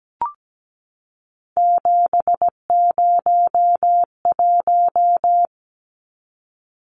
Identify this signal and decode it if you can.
Morse code, heard as a keyed tone, about 17 wpm, 700 Hz. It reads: 701
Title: Numbers - 3-Digit 17wpm